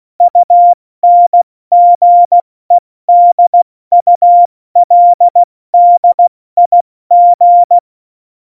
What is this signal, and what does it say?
Morse code, heard as a keyed tone, about 16 wpm, spaced as for 14 wpm, 700 Hz. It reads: UNGEDULDIG
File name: Morsekurs / Witze / Witz 13 March 2018, 70 F80